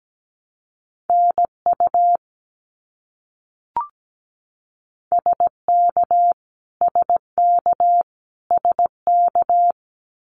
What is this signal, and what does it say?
Morse code, heard as a keyed tone, about 17 wpm, 700 Hz. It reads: NU SK SK SK
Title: Canadian Provinces and Territories - Repeated 3x - 17wpm